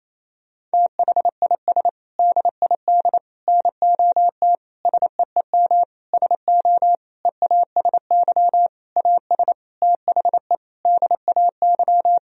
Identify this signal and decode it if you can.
Morse code, heard as a keyed tone, about 28 wpm, 700 Hz. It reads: THIS DID NOT SEEM SO EASY AS THE DAY
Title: Sentences from Top 1000 Words - Review Part 1 - 28wpm